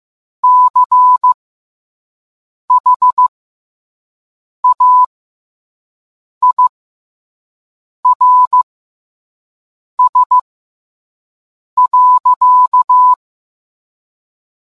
Morse code, heard as a keyed tone, about 15 wpm, spaced as for 5 wpm, 1000 Hz. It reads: CHAIRS.